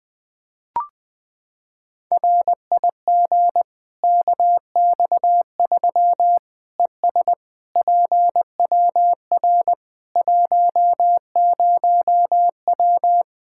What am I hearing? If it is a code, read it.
Morse code, heard as a keyed tone, about 20 wpm, 700 Hz. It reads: RIG KX3 ES PWR 10W
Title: Practice QSO - Exchange 2 - Info - 20wpm